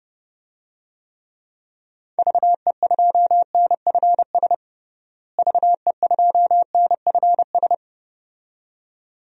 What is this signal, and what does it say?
Morse code, heard as a keyed tone, about 32 wpm, 700 Hz. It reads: VE2NFS VE2NFS
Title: DXCC-100 Call Signs - 6 Characters - Repeated 2x - 30wpm